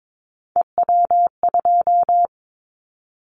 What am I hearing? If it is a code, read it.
Morse code, heard as a keyed tone, about 22 wpm, 700 Hz. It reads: EW2
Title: Callsign Prefixes for Top 100 DXCC - 22wpm